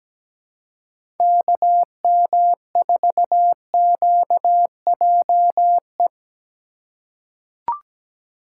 Morse code, heard as a keyed tone, about 17 wpm, 700 Hz. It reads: KM4QJE
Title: USA Call Signs - 6 Characters - 4x Character Spacing - 17wpm